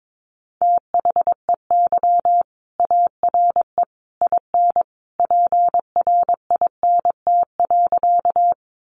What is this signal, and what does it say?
Morse code, heard as a keyed tone, about 22 wpm, 700 Hz. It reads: THEY ARE IN PRINT.
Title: Sentences from Top 1400 Words 22wpm